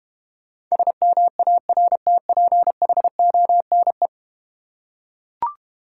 Morse code, heard as a keyed tone, about 32 wpm, 700 Hz. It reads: SMARTPHONE